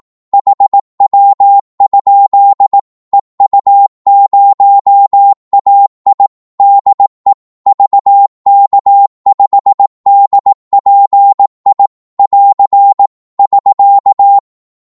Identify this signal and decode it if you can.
Morse code, heard as a keyed tone, about 18 wpm, 800 Hz. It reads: HW? EU0AI DE VK5DPI <AR> <SK>
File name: Morse / qso-18wpm